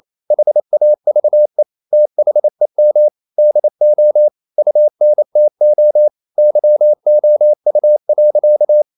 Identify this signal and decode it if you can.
Morse code, heard as a keyed tone, about 28 wpm, 600 Hz. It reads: HAVE THEM DO UNTO YOU.
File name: CW / Practice / EnglishSayings28